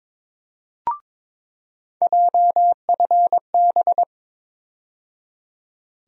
Morse code, heard as a keyed tone, about 22 wpm, 700 Hz. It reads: JFB
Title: USA Call Sign Postfixes - 3 Letters 22wpm